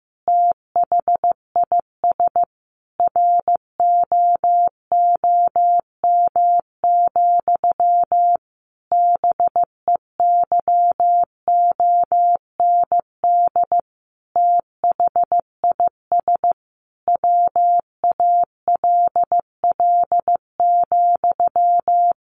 Morse code, heard as a keyed tone, about 15 wpm, 700 Hz. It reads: THIS ROOM, BEYOND THIS WALL,
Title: A Sound Of Thunder 15wpm